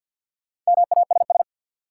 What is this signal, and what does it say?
Morse code, heard as a keyed tone, about 48 wpm, 700 Hz. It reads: MASS
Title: Sets of 5 Words Encore - New Word Drill - 50wpm